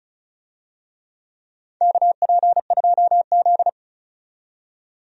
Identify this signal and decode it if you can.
Morse code, heard as a keyed tone, about 35 wpm, 700 Hz. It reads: KP2Z
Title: DXCC-100 Call Signs - 4 Characters - 35wpm